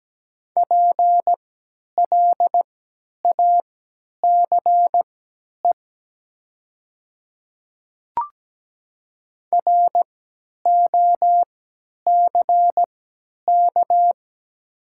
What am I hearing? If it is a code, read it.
Morse code, heard as a keyed tone, about 17 wpm, spaced as for 10 wpm, 700 Hz. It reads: PLACE ROCK
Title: 3x Character Spacing - Top 500 Words - 17wpm